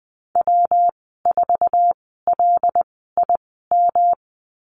Morse code, heard as a keyed tone, about 20 wpm, spaced as for 15 wpm, 700 Hz. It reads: W4LIM
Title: 147 Lesson 24g - Letter y - Call Signs 20wpm